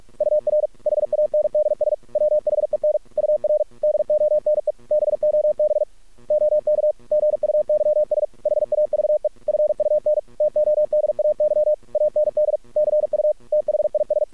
Morse code, heard as a keyed tone, about 48 wpm, 610 Hz. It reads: UR HANDS WHEN UR DONE BOB OK GUYS HAVE FUN TODAY AND CU THIS
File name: archived / CRSnet